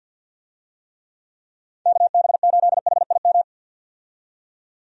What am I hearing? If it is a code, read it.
Morse code, heard as a keyed tone, about 51 wpm, 700 Hz. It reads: KB8HIM
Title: USA Call Signs - 6 Characters - 50wpm